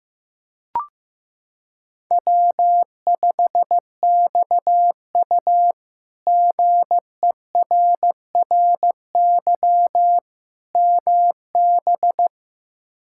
Morse code, heard as a keyed tone, about 15 wpm, 700 Hz. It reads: W5XU GERRY MB
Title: K1USN SST QSO Practice - 15wpm